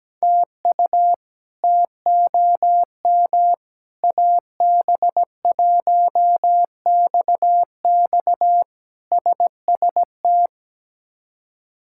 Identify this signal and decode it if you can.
Morse code, heard as a keyed tone, about 17 wpm, 700 Hz. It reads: TU TOM AB1XX SST
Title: K1USN SST QSO Practice - 17wpm